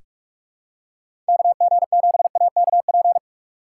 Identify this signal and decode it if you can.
Morse code, heard as a keyed tone, about 46 wpm, 700 Hz. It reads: KG7AKP
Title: USA Call Signs - 6 Characters - 2x Character Spacing - 45wpm